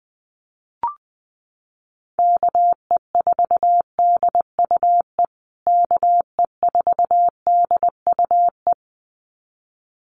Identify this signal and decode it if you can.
Morse code, heard as a keyed tone, about 19 wpm, 700 Hz. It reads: KE4DUE KE4DUE
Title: USA Call Signs Repeated 2x 20wpm